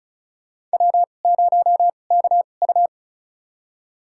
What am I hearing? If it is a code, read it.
Morse code, heard as a keyed tone, about 34 wpm, spaced as for 25 wpm, 690 Hz. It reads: W0KU
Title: USA Call Signs - 4 Characters - 2x Character Spacing - 35wpm